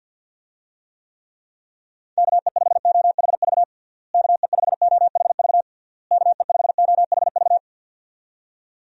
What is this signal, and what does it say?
Morse code, heard as a keyed tone, about 52 wpm, 700 Hz. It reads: KE5OHV KE5OHV KE5OHV
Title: USA Call Signs - 6 Characters - Repeated 3x - 50wpm